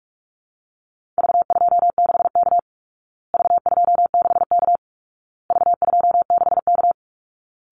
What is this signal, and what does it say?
Morse code, heard as a keyed tone, about 44 wpm, 700 Hz. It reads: V26K V26K V26K
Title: Call Signs Repeated 2x - 45wpm